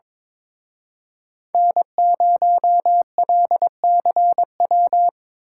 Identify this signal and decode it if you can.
Morse code, heard as a keyed tone, about 25 wpm, 700 Hz. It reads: N0LCW
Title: Speed Racing - USA Call Signs 22wpm